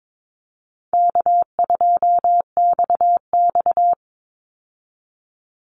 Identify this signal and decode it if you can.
Morse code, heard as a keyed tone, about 22 wpm, 700 Hz. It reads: K2XX
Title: Easy Call Signs - 22wpm